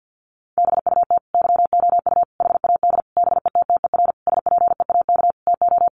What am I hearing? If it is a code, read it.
Morse code, heard as a keyed tone, about 50 wpm, 700 Hz. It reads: BUT YOU HAD BETTER SPEAK TO
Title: Sentences from Top 1100 Words 50wpm